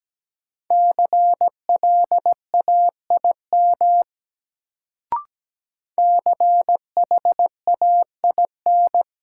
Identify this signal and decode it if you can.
Morse code, heard as a keyed tone, about 17 wpm, 700 Hz. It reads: CLAIM CHAIN
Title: Sentences from Top 1500 Words - New Word Drill Alt - 17wpm